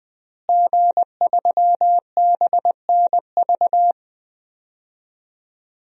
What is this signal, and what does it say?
Morse code, heard as a keyed tone, about 20 wpm, 700 Hz. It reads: G3BNV
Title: DXCC-100 Call Signs - 5 Characters - Repeated 3x - 20wpm